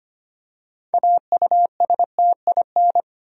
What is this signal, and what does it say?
Morse code, heard as a keyed tone, about 25 wpm, 700 Hz. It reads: AUSTIN
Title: USA Common Names - 25wpm